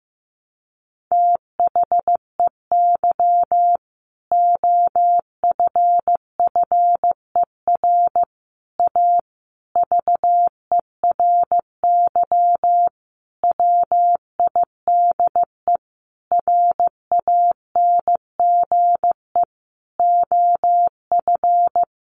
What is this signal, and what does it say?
Morse code, heard as a keyed tone, about 15 wpm, 700 Hz. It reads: THEY OFFER A VERY WIDE RANGE OF
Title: Sentences from Top 900 Words 15wpm